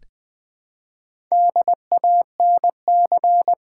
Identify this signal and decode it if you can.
Morse code, heard as a keyed tone, about 20 wpm, 700 Hz. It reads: DANC
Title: Build A Word - 1 Letter At A Time - 5-Letter Words - 20wpm